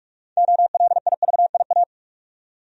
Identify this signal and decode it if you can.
Morse code, heard as a keyed tone, about 47 wpm, 700 Hz. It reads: OLIVIA
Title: USA Common Names - 45wpm